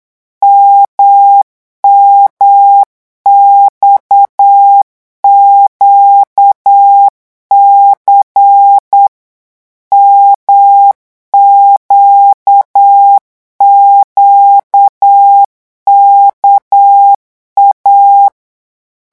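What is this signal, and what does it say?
Morse code, heard as a keyed tone, about 8 wpm, 800 Hz. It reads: MMXQC MQQKA
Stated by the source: Morse code revision lesson at 8 words per minute, from a French amateur radio course.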